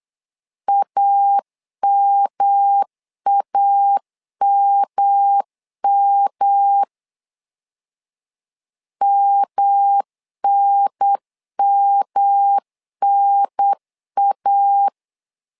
Morse code, heard as a keyed tone, about 8 wpm, 800 Hz. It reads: AMAMM MNMNA
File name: LEZIONI / 3 MWRO / 3 M 8wpm 2300